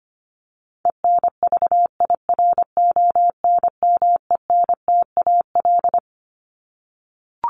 Morse code, heard as a keyed tone, about 25 wpm, 700 Hz. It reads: ENVIRONMENTAL
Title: Top 1001-1250 Words 25wpm